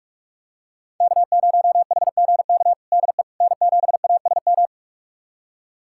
Morse code, heard as a keyed tone, about 46 wpm, 700 Hz. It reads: K0HGK DE N7ASM
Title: USA Callsign DE Callsign - 5 Characters - 45wpm